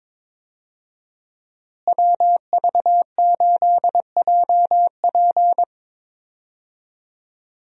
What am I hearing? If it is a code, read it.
Morse code, heard as a keyed tone, about 22 wpm, 700 Hz. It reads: WV8JP
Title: USA Call Signs - 5 Characters - Repeated 3x - 22wpm